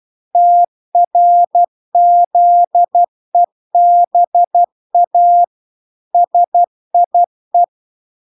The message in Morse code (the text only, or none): TRZEBA SIE